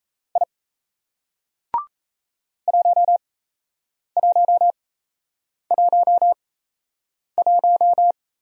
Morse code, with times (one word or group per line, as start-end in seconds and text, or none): 0.4-0.4 I
2.7-8.1 1111